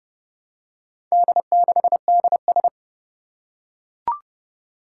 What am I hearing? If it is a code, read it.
Morse code, heard as a keyed tone, about 30 wpm, 700 Hz. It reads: D6DS
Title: Four Challenging Letter-Number Combinations 30wpm